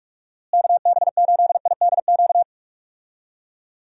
Morse code, heard as a keyed tone, about 46 wpm, 700 Hz. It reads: KB8IDQ
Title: USA Call Signs - 6 Characters - Repeated 3x - 45wpm